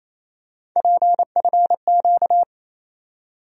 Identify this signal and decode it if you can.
Morse code, heard as a keyed tone, about 28 wpm, 700 Hz. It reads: PFQ
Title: USA Call Sign Postfixes - 3 Letters 28wpm